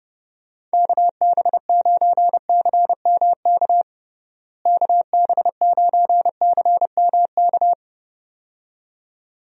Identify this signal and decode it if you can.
Morse code, heard as a keyed tone, about 32 wpm, 700 Hz. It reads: KB9CMK KB9CMK
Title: USA Call Signs - 6 Characters - Repeated 2x - 30wpm